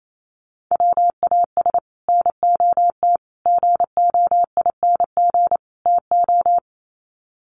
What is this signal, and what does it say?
Morse code, heard as a keyed tone, about 28 wpm, 700 Hz. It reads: WAS NOT GOING TO
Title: Sets of 4 Words 28wpm